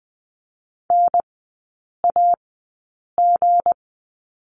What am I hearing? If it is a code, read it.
Morse code, heard as a keyed tone, about 20 wpm, spaced as for 8 wpm, 700 Hz. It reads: NAG
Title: 184 Lesson 30e - Letter b - Three Characters 20wpm